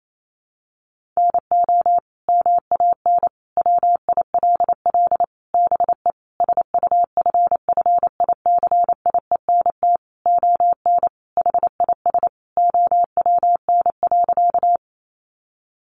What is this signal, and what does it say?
Morse code, heard as a keyed tone, about 28 wpm, 700 Hz. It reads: NO MAN WILL BE SUFFICIENT ON HIS OWN.
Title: Sentences from Top 1400 Words 28wpm